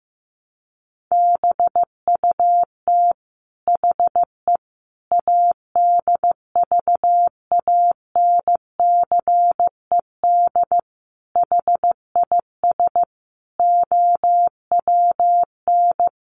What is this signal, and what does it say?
Morse code, heard as a keyed tone, about 15 wpm, 700 Hz. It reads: BUT HE ADVANCED HIS OWN
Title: Sentences from Top 1200 Words 15wpm